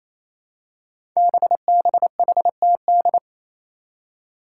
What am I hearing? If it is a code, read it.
Morse code, heard as a keyed tone, about 27 wpm, 700 Hz. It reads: BBHTD
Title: Five Challenging Letter-Number Combinations 28wpm